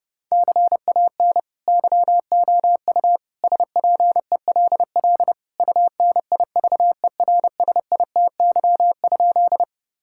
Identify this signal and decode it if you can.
Morse code, heard as a keyed tone, about 32 wpm, 700 Hz. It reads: CAN YOU SPELL UNIVERSITY?
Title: Sentences from Top 1000 Words - Review Part 5 - 30wpm